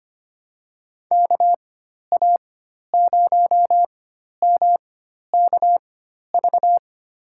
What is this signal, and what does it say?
Morse code, heard as a keyed tone, about 25 wpm, spaced as for 12 wpm, 700 Hz. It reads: KA0MKV
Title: USA Call Signs - 6 Characters - 4x Character Spacing - 25wpm